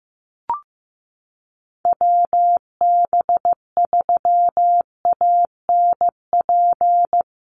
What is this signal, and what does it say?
Morse code, heard as a keyed tone, about 15 wpm, 700 Hz. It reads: WB3ANP
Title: USA Call Signs Repeated 2x 15wpm